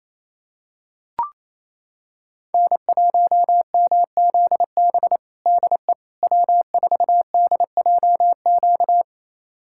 Morse code, heard as a keyed tone, about 28 wpm, 700 Hz. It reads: N1MZB DE W4DJQ
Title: USA Callsign DE Callsign - 5 Characters - 28wpm